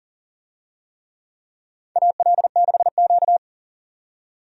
Morse code, heard as a keyed tone, about 40 wpm, 700 Hz. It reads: AL6Q
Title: USA Call Signs - 4 Characters - 40wpm